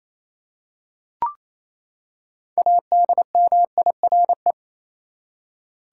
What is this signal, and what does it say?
Morse code, heard as a keyed tone, about 28 wpm, 700 Hz. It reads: ADMIRE
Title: Six Letter Words - 28wpm